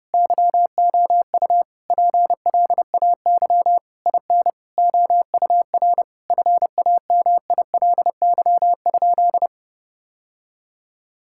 Morse code, heard as a keyed tone, about 30 wpm, 700 Hz. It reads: YOU PLAY IN OUR FAMILY?
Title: Sentences from Top 1000 Words - Review Part 6 - 30wpm